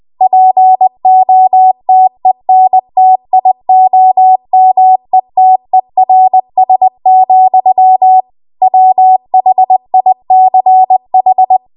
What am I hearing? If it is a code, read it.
Morse code, heard as a keyed tone, about 20 wpm, 750 Hz. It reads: POTENTIOMETERS, WHICH